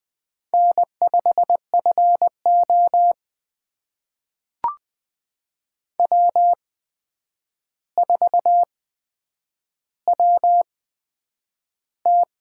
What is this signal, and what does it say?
Morse code, heard as a keyed tone, about 20 wpm, spaced as for 9 wpm, 700 Hz. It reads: N5FO W4WT
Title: USA Call Signs - 4 Characters - 8x Character Spacing - 20wpm